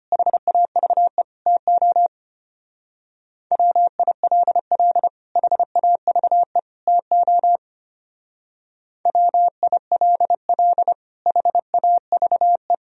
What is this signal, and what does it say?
Morse code, heard as a keyed tone, about 29 wpm, 700 Hz. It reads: HAVE TO WILL HAVE TO WILL HAVE
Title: Speed-Racing - Sets of 3 Words 25wpm